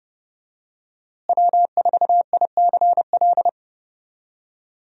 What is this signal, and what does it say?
Morse code, heard as a keyed tone, about 30 wpm, 700 Hz. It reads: W4ICL
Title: USA Call Signs - 5 Characters - Repeated 3x - 30wpm